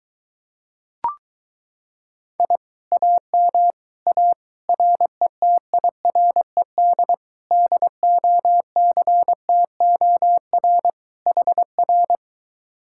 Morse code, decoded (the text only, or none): I AM A RETIRED DOCTOR HR